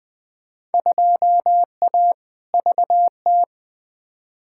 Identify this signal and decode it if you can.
Morse code, heard as a keyed tone, about 20 wpm, 700 Hz. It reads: 2A VT
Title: Field Day Exchanges Encore - 20wpm